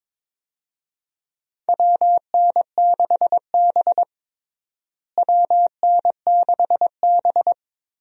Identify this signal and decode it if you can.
Morse code, heard as a keyed tone, about 22 wpm, 700 Hz. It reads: WN6B WN6B
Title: USA Call Signs - 4 Characters - Repeated 2x - 22wpm